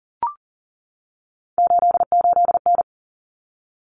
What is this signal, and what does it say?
Morse code, heard as a keyed tone, about 40 wpm, 700 Hz. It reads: 88N
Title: Common Cut Numbers - 3-Digit 40wpm